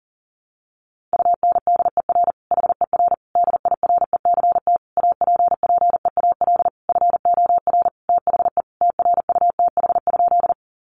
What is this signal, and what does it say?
Morse code, heard as a keyed tone, about 40 wpm, 700 Hz. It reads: UNDER HER DIRECT APPEAL FOR THE TRUTH?
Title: Sentences from Top 1300 Words 40wpm